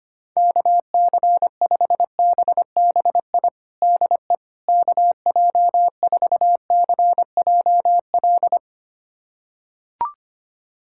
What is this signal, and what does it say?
Morse code, heard as a keyed tone, about 25 wpm, 700 Hz. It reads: KC5BBI DE KJ4CJL
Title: USA Callsign DE Callsign - 6 Characters - 25wpm